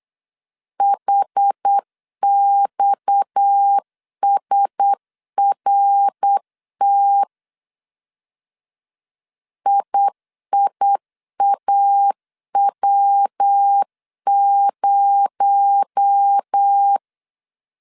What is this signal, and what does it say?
Morse code, heard as a keyed tone, about 8 wpm, 800 Hz. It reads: HXSRT IIAW0
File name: 8wpm